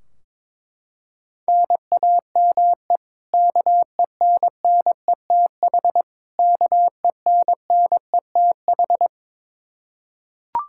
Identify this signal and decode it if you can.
Morse code, heard as a keyed tone, about 22 wpm, 700 Hz. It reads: NAME KENNETH KENNETH
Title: Practice QSO - Exchange 1 - Beginning - 22wpm